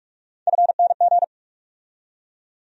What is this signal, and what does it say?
Morse code, heard as a keyed tone, about 41 wpm, 700 Hz. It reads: FNG
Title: USA Call Sign Postfixes - 3 Letters 45wpm